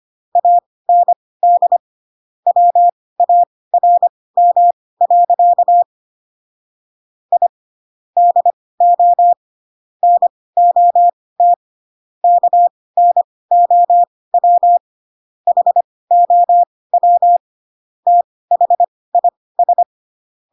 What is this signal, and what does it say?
Morse code, heard as a keyed tone, about 25 wpm, spaced as for 18 wpm, 700 Hz. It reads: AND WARM. I DO NOT KNOW HOW THIS
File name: Pinocchio / Pinocchio ch1 12 25 ik1hge